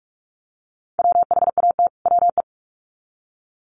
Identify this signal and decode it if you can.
Morse code, heard as a keyed tone, about 44 wpm, 700 Hz. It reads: WHAT WE
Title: Sets of 2 Words 45wpm